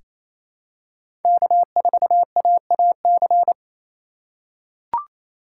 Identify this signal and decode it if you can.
Morse code, heard as a keyed tone, about 28 wpm, 700 Hz. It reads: K4AAC
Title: K1USN SST QSO Practice - 28wpm